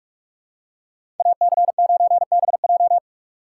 Morse code, heard as a keyed tone, about 46 wpm, 700 Hz. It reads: AC9BJ
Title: USA Call Signs - 5 Characters - 4x Character Spacing - 45wpm